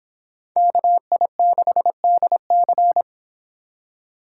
Morse code, heard as a keyed tone, about 30 wpm, 700 Hz. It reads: KI6DC